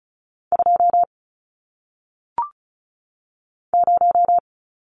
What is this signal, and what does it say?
Morse code, heard as a keyed tone, about 36 wpm, 700 Hz. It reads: 2 0
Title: Numbers - 1-Digit 35wpm